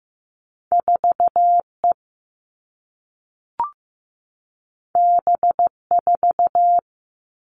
Morse code, heard as a keyed tone, about 15 wpm, 700 Hz. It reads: 4E B4
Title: All Cut Numbers - 2-Digit 15wpm